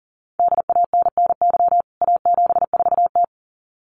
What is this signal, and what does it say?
Morse code, heard as a keyed tone, about 40 wpm, 700 Hz. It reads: DANNY A74T
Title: CWT Exchange 40wpm